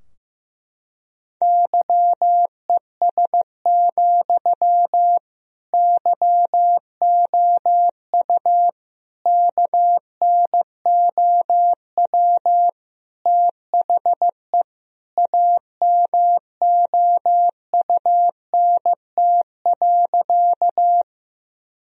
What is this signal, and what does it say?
Morse code, heard as a keyed tone, about 14 wpm, 700 Hz. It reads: YES, YOU KNOW THE AMOUNT.
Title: Sentences from Top 1000 Words - Review Part 4 - 15wpm